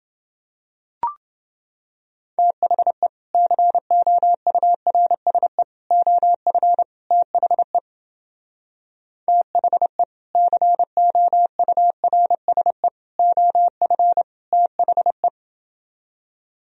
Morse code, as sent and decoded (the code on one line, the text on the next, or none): - .... / -.-. --- ..- .-. ... / --- ..-. / - .... / - .... / -.-. --- ..- .-. ... / --- ..-. / - .... .
THE COURSE OF THE THE COURSE OF THE